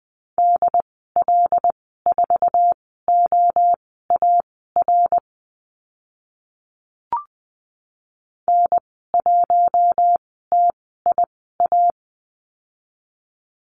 Morse code, heard as a keyed tone, about 19 wpm, spaced as for 14 wpm, 700 Hz. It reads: DL4OAR N1TIA
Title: 082 Lesson 14g - Number 2 - Call Signs 20wpm